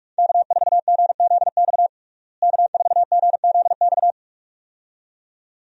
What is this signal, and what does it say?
Morse code, heard as a keyed tone, about 46 wpm, 700 Hz. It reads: K4GZX K4GZX
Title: USA Call Signs - 5 Characters - Repeated 2x - 45wpm